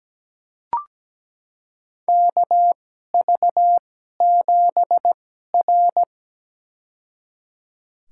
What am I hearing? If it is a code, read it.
Morse code, heard as a keyed tone, about 17 wpm, spaced as for 12 wpm, 700 Hz. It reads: KV7R
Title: USA Call Signs - 4 Characters - 2x Character Spacing - 17wpm